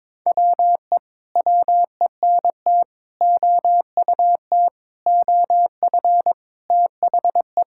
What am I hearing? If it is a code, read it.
Morse code, heard as a keyed tone, about 22 wpm, 700 Hz. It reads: WE WENT OUT OF THE
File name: Sentences with 9-Word Phrases - 22wpm